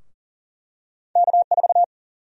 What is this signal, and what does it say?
Morse code, heard as a keyed tone, about 40 wpm, 700 Hz. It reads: K4
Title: USA Call Sign Prefixes - Repeated 2x - 40wpm